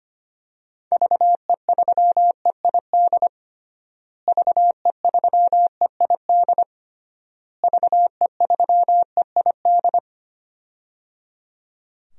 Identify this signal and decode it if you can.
Morse code, heard as a keyed tone, about 25 wpm, 700 Hz. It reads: VE3EID VE3EID VE3EID
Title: DXCC-100 Call Signs - 6 Characters - Repeated 3x - 25wpm